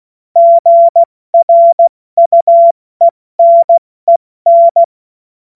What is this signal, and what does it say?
Morse code, heard as a keyed tone, about 16 wpm, 680 Hz. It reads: GRUENEN
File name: Morsekurs / Witze / Witz 28. September 2021, 70 F80